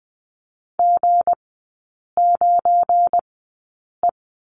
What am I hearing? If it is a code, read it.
Morse code, heard as a keyed tone, about 20 wpm, spaced as for 8 wpm, 700 Hz. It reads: G9E
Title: 218 Lesson 35e - Number 0 - Three Characters 20wpm